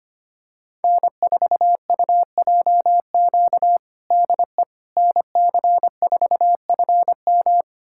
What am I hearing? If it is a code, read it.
Morse code, heard as a keyed tone, about 25 wpm, 700 Hz. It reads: N4UJQ DE NC4FM
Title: USA Callsign DE Callsign - 5 Characters - 25wpm